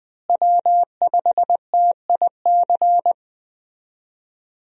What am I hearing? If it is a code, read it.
Morse code, heard as a keyed tone, about 20 wpm, 700 Hz. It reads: W5TIC
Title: USA Call Signs - 5 Characters - Repeated 2x - 20wpm